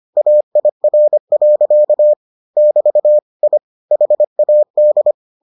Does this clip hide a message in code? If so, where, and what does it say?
Yes: Morse code from 0.2 to 5.1 s, heard as a keyed tone, about 25 wpm, 600 Hz. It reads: AIR. <BT> I HAD